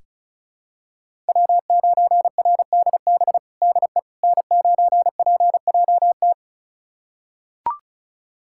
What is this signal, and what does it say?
Morse code, heard as a keyed tone, about 36 wpm, 700 Hz. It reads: W9RDB DE N9PJT
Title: USA Callsign DE Callsign - 5 Characters - 35wpm